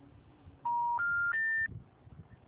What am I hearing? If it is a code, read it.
Special information tone.